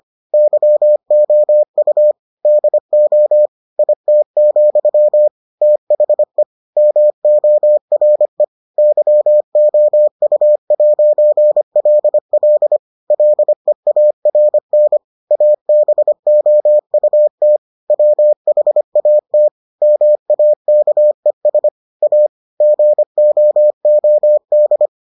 Morse code, heard as a keyed tone, about 25 wpm, 600 Hz. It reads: YOU DO IT, THE MORE YOU'LL LEARN ABOUT WHAT MAKES A GOOD